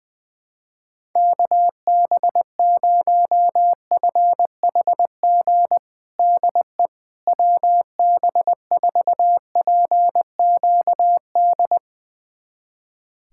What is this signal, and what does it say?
Morse code, heard as a keyed tone, about 21 wpm, 700 Hz. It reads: KB0FHG DE WB4PQD
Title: USA Callsign DE Callsign - 6 Characters - 20wpm